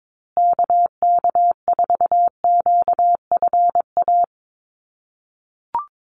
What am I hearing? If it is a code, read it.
Morse code, heard as a keyed tone, about 22 wpm, 700 Hz. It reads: KK4QFA
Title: USA Call Signs Repeated 3x 22wpm